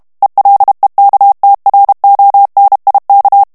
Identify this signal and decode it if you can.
Morse code, heard as a keyed tone, about 32 wpm, 800 Hz. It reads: ELEKTRONIK
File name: ALLEMAND texte clair 1 vitesse 30 mots minute